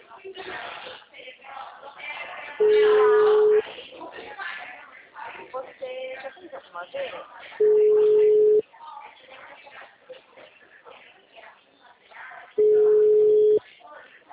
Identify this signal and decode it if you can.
Ringback tone (Europe, 425 Hz), with speech.
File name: out- -2016- - - .370